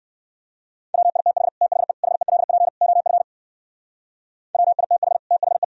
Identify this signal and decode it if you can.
Morse code, heard as a keyed tone, about 53 wpm, 700 Hz. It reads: WITH THE HELP OF WITH THE
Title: Speed-Racing - Sets of 4 Words 45wpm